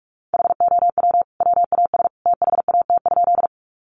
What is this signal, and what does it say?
Morse code, heard as a keyed tone, about 44 wpm, 700 Hz. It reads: HOW WAS THAT?
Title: Sentences from Top 100 Words 45wpm